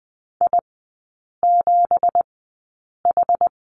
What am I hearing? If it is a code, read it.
Morse code, heard as a keyed tone, about 20 wpm, spaced as for 8 wpm, 690 Hz. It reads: I7H
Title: 198 Lesson 32e - Letter k - Three Characters 20wpm